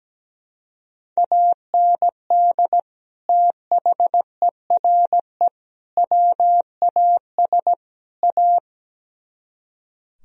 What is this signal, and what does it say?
Morse code, heard as a keyed tone, about 17 wpm, 700 Hz. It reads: AND THERE WAS A
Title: Speed-Racing - Sets of 4 Words 17wpm